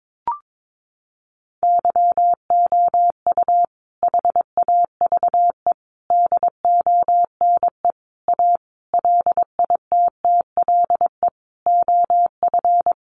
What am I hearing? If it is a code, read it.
Morse code, heard as a keyed tone, about 22 wpm, 700 Hz. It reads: YOU HAVE DONE A LITTLE OF
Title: Sentences from Top 700 Words 22wpm